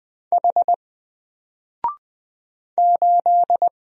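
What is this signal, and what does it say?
Morse code, heard as a keyed tone, about 20 wpm, 700 Hz. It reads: H 8
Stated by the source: Single letter and number Morse drill at 20 wpm, from N4LCD.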